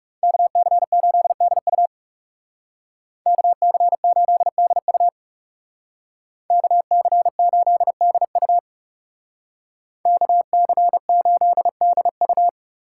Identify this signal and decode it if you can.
Morse code, heard as a keyed tone, about 35 wpm, 700 Hz. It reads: KC8DU KC8DU KC8DU KC8DU
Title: Speed Racing - USA 2x2 Call Signs 30wpm